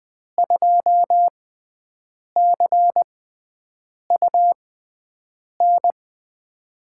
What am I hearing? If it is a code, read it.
Morse code, heard as a keyed tone, about 20 wpm, spaced as for 7 wpm, 700 Hz. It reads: 2CUN
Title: USA Call Signs - 5 Characters - 6x Character Spacing - 20wpm